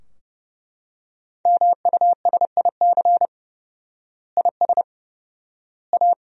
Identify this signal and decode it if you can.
Morse code, heard as a keyed tone, about 30 wpm, 700 Hz. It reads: MUSIC IS A
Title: 4x Word-Spacing - Sentences from Top 300 Words - 30wpm